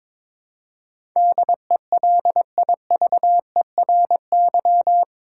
Morse code, heard as a keyed tone, about 22 wpm, 700 Hz. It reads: DELIVERY
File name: practice / 22wpm / Eight Letter Words - 22wpm